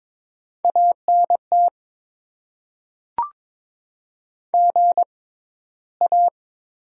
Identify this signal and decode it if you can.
Morse code, heard as a keyed tone, about 21 wpm, spaced as for 13 wpm, 700 Hz. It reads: ANT GA
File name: Top 20 QSO Elements - 6x Character Spacing - 22wpm